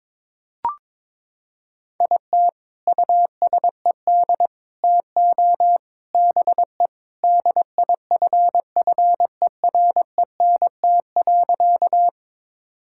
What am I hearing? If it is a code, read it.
Morse code, heard as a keyed tone, about 23 wpm, 700 Hz. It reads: IT USED TO BE DIFFERENT.
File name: practice / 22wpm / Sentences with 4-Word Phrases - 22wpm